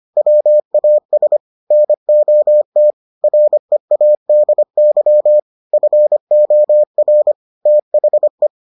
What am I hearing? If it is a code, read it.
Morse code, heard as a keyed tone, about 25 wpm, 600 Hz. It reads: WAS NOT READY FOR THE